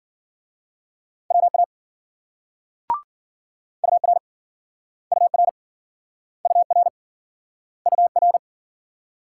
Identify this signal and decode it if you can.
Morse code, heard as a keyed tone, about 56 wpm, 700 Hz. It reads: WA UR UR UR UR